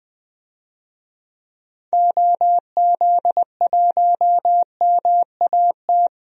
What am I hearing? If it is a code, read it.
Morse code, heard as a keyed tone, about 20 wpm, 700 Hz. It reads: OZ1MAT
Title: DXCC-100 Call Signs - 6 Characters - 20wpm